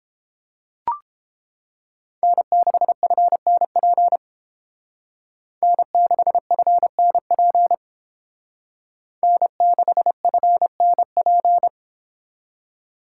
Morse code, heard as a keyed tone, about 31 wpm, 700 Hz. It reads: N6FNP N6FNP N6FNP